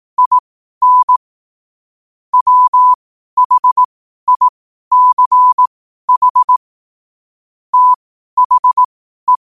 Morse code, heard as a keyed tone, about 18 wpm, spaced as for 12 wpm, 1000 Hz. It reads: IN WHICH THE
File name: qotdfiles / 12wpm